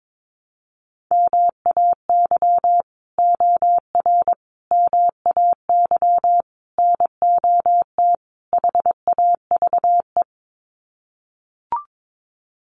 Morse code, heard as a keyed tone, about 22 wpm, 700 Hz. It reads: MAY OR MAY NOT HAVE
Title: Sets of 5 Words Encore2 22wpm